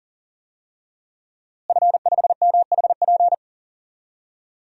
Morse code, heard as a keyed tone, about 40 wpm, 700 Hz. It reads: F5MHP
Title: DXCC-100 Call Signs - 5 Characters - Repeated 2x - 40wpm